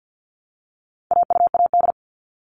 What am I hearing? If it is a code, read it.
Morse code, heard as a keyed tone, about 50 wpm, 700 Hz. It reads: AUAD
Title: All Cut Numbers - 4-Digit 50wpm